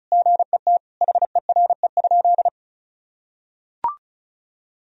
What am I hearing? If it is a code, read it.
Morse code, heard as a keyed tone, about 34 wpm, 700 Hz. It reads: GET HERE?
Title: Sentences from Top 1000 Words - Review Part 1 - 35wpm